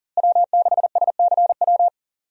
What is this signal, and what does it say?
Morse code, heard as a keyed tone, about 39 wpm, 700 Hz. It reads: W6SCW
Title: USA Call Signs - 5 Characters - Repeated 2x - 40wpm